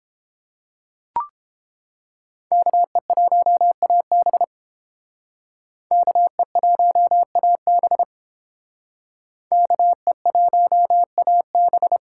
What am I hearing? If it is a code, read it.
Morse code, heard as a keyed tone, about 29 wpm, 700 Hz. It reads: KE1AB KE1AB KE1AB